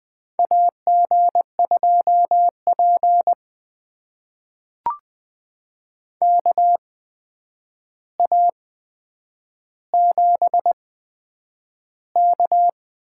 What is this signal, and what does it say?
Morse code, heard as a keyed tone, about 20 wpm, spaced as for 9 wpm, 700 Hz. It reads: AG2P KA7K
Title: USA Call Signs - 4 Characters - 8x Character Spacing - 20wpm